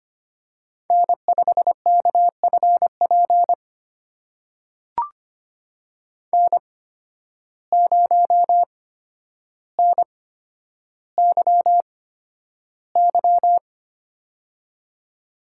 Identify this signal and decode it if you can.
Morse code, heard as a keyed tone, about 25 wpm, spaced as for 11 wpm, 700 Hz. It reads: N5KFP N0NYY
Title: USA Call Signs - 5 Characters - 8x Character Spacing - 25wpm